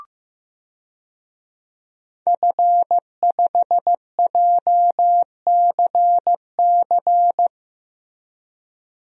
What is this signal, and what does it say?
Morse code, heard as a keyed tone, about 15 wpm, 700 Hz. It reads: F5JCC
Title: DXCC-100 Call Signs - 5 Characters - Repeated 3x - 15wpm